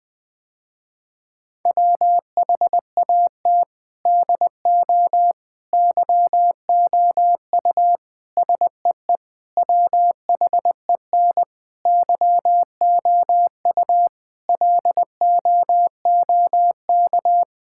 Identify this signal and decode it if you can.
Morse code, heard as a keyed tone, about 20 wpm, 700 Hz. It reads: WHAT DO YOU SEE WHEN YOU LOOK
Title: Sentences with 4-Word Phrases - Spoken Beginning - 20wpm